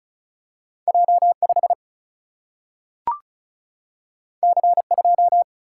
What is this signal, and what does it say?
Morse code, heard as a keyed tone, about 36 wpm, 700 Hz. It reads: J5 C2
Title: Single-Letter Single-Number 35wpm